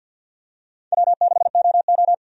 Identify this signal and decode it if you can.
Morse code, heard as a keyed tone, about 48 wpm, 700 Hz. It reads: W6OO
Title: USA Call Signs - 4 Characters - 50wpm